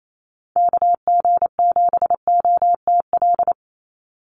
Morse code, heard as a keyed tone, about 28 wpm, 700 Hz. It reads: KG7OTL
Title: USA Call Signs 28wpm